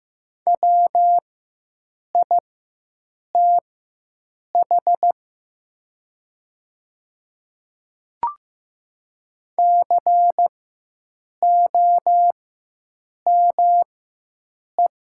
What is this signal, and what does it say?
Morse code, heard as a keyed tone, about 15 wpm, spaced as for 7 wpm, 700 Hz. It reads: WITH COME
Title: 4x Character Spacing - Top 100 Words - 15wpm